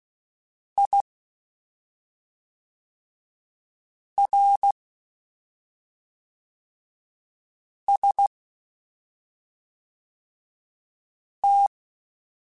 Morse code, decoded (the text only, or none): IRST